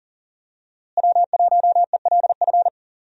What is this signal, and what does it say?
Morse code, heard as a keyed tone, about 38 wpm, 700 Hz. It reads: W1ELF
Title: USA Call Signs - 5 Characters - Repeated 2x - 40wpm